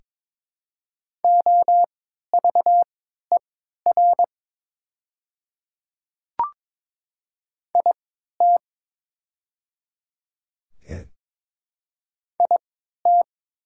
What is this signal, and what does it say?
Morse code, heard as a keyed tone, about 21 wpm, spaced as for 12 wpm, 700 Hz. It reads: OVER IT IT
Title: 3x Character Spacing - Top 100 Words - 22wpm